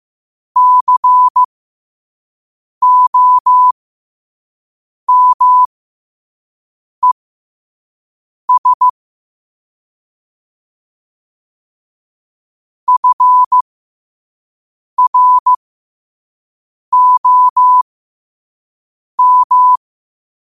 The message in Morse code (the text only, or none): COMES FROM